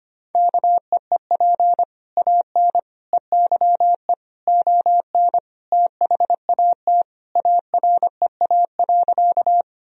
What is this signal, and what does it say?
Morse code, heard as a keyed tone, about 25 wpm, 700 Hz. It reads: KEEP AN EYE ON THAT AREA.
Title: Sentences from Top 1000 Words - Review Part 2 - 25wpm